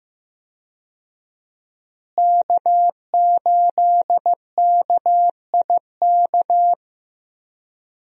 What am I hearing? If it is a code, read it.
Morse code, heard as a keyed tone, about 15 wpm, 700 Hz. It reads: K8KIK
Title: USA Call Signs - 5 Characters - Repeated 2x - 15wpm